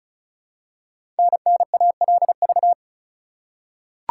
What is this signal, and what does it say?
Morse code, heard as a keyed tone, about 34 wpm, 700 Hz. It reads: NNALV
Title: Code Groups - 5 Letter:Numbers - 35wpm